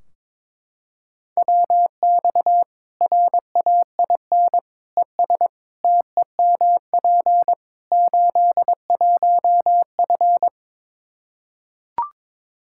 Morse code, heard as a keyed tone, about 22 wpm, 700 Hz. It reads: WX RAIN ES TEMP 81F
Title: Practice QSO - Exchange 2 - Info - 22wpm